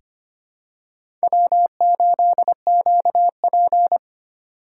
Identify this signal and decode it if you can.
Morse code, heard as a keyed tone, about 25 wpm, 700 Hz. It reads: W8QP
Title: USA Call Signs - 4 Characters - 8x Character Spacing - 25wpm